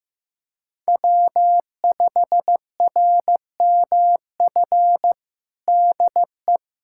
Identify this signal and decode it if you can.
Morse code, heard as a keyed tone, about 16 wpm, 700 Hz. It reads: W5RMF DE
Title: USA Callsign DE Callsign 15wpm